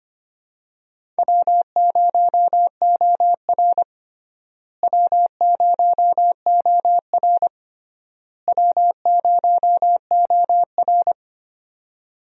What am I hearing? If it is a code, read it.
Morse code, heard as a keyed tone, about 25 wpm, 700 Hz. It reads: W0OR W0OR W0OR
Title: USA Call Signs - 4 Characters - Repeated 3x - 25wpm